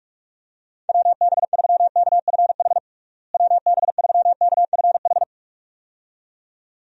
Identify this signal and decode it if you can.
Morse code, heard as a keyed tone, about 42 wpm, 700 Hz. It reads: WB3KFH WB3KFH
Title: USA Call Signs - 6 Characters - Repeated 2x - 45wpm